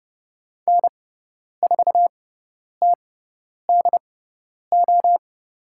Morse code, heard as a keyed tone, about 30 wpm, spaced as for 10 wpm, 700 Hz. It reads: N4TDO
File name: Lesson 17g - Letter u - Call Signs 30wpm